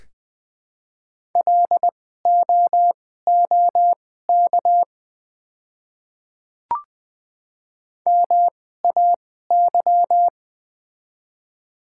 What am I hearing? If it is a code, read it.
Morse code, heard as a keyed tone, about 20 wpm, spaced as for 14 wpm, 700 Hz. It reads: LOOK MAY
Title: 2x Character Spacing - Top 100 Words - 20wpm